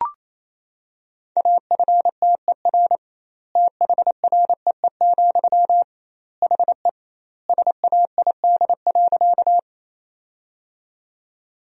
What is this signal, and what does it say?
Morse code, heard as a keyed tone, about 28 wpm, 700 Hz. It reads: AFTER THREE, HE SAID.
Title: 2x Word-Spacing - Sentences from Top 200 Words - 28wpm